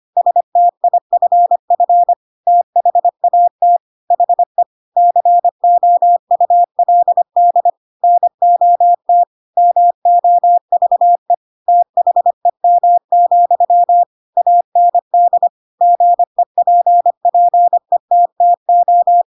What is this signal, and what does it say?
Morse code, heard as a keyed tone, about 25 wpm, 700 Hz. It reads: STIFF THAT HE COULD NOT MOVE THEM, AND GEPPETTO
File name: Pinocchio / Pinocchio ch3 25 25 ik1hge